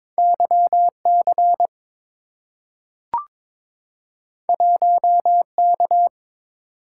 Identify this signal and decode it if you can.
Morse code, heard as a keyed tone, about 22 wpm, 700 Hz. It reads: YC 1K
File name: practice / 22wpm / Two Letter-Number Combinations - 22wpm